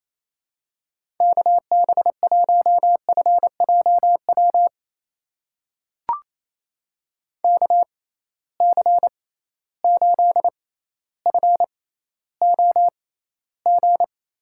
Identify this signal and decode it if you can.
Morse code, heard as a keyed tone, about 28 wpm, spaced as for 14 wpm, 700 Hz. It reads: KB1FJW KC8FOG
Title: USA Call Signs - 6 Characters - 6x Character Spacing - 28wpm